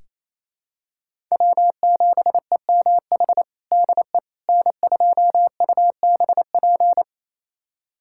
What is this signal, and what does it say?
Morse code, heard as a keyed tone, about 28 wpm, 700 Hz. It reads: W7EMH DE N2UBP
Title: USA Callsign DE Callsign - 5 Characters - 28wpm